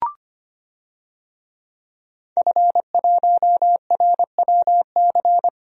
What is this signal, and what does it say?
Morse code, heard as a keyed tone, about 25 wpm, 700 Hz. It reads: F1RWC
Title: DXCC-100 Call Signs - 5 Characters - Repeated 3x - 25wpm